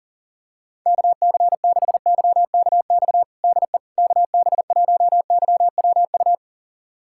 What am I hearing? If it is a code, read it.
Morse code, heard as a keyed tone, about 40 wpm, 700 Hz. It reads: KC6YKX DE KB1YWU
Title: USA Callsign DE Callsign - 6 Characters - 40wpm